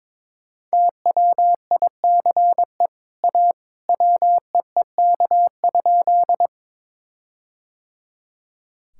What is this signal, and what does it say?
Morse code, heard as a keyed tone, about 22 wpm, 700 Hz. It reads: TWICE A WEEK?
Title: Sentences from Top 1600 Words 22wpm